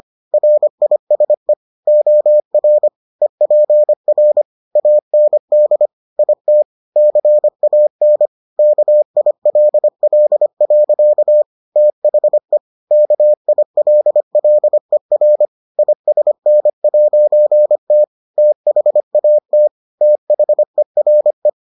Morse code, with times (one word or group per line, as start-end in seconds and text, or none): 0.3-1.5 RISE
1.9-2.9 OR
3.2-4.4 EPR
4.8-5.9 AND
6.2-6.6 IT
7.0-8.3 CAN
8.6-11.4 KILL.
11.8-12.6 THE
12.9-15.4 KILLER
15.8-18.0 ISN'T
18.4-19.7 THAT
20.0-21.5 THERE